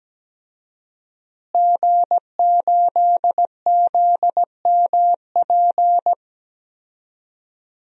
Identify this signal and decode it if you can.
Morse code, heard as a keyed tone, about 17 wpm, 700 Hz. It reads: G8ZMP
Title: DXCC-100 Call Signs - 5 Characters - Repeated 3x - 17wpm